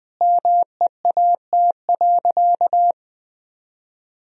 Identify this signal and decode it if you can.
Morse code, heard as a keyed tone, about 21 wpm, 700 Hz. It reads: MEAT.
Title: Sentences from Top 1000 Words - Review Part 3 - 20wpm